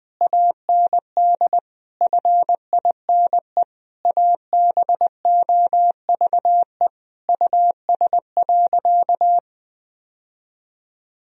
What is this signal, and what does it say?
Morse code, heard as a keyed tone, about 21 wpm, 700 Hz. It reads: AND FINE ABOVE US.
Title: Sentences from Top 1000 Words - Review Part 2 - 20wpm